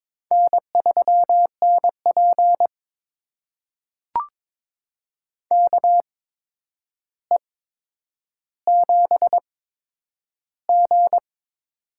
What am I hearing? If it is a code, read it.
Morse code, heard as a keyed tone, about 22 wpm, spaced as for 9 wpm, 700 Hz. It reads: N3NP KE7G
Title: USA Call Signs - 4 Characters - 8x Character Spacing - 22wpm